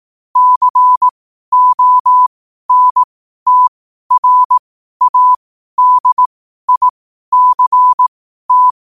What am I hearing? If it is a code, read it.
Morse code, heard as a keyed tone, about 18 wpm, spaced as for 12 wpm, 1000 Hz. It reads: CONTRADICT
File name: qotdfiles / 12wpm